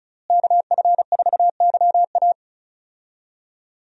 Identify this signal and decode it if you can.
Morse code, heard as a keyed tone, about 36 wpm, 700 Hz. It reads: KF4YA
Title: USA Call Signs - 5 Characters - Repeated 3x - 35wpm